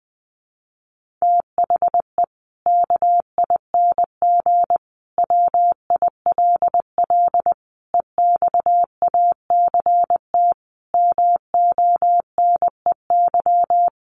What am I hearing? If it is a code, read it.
Morse code, heard as a keyed tone, about 20 wpm, 700 Hz. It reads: THE KING WILL EXACT MONEY
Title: Sentences from Top 1300 Words 20wpm